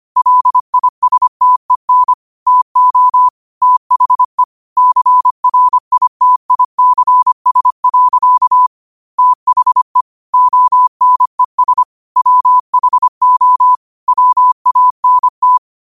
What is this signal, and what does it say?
Morse code, heard as a keyed tone, about 25 wpm, 1000 Hz. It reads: LISTEN TO THE CRITICS. THE ONES WHO WANT
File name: qotdfiles / 25wpm